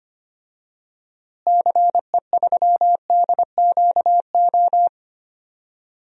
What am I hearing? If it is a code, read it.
Morse code, heard as a keyed tone, about 25 wpm, 700 Hz. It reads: CE3DQO
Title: DXCC-100 Call Signs - 6 Characters - Repeated 2x - 25wpm